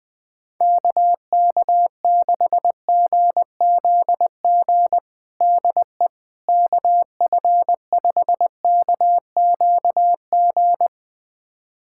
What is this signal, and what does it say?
Morse code, heard as a keyed tone, about 20 wpm, 700 Hz. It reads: KK6GZG DE KF5KQG
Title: USA Callsign DE Callsign - 6 Characters - 20wpm